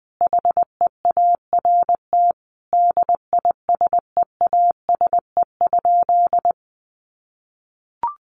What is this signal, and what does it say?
Morse code, heard as a keyed tone, about 20 wpm, 700 Hz. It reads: HEART DISEASE?
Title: Sentences from Top 1200 Words 20wpm